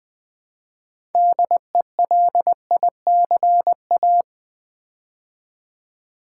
Morse code, heard as a keyed tone, about 20 wpm, 700 Hz. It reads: DELICA
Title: Build A Word - 3 Letters At A Time - 8-Letter Words - 20wpm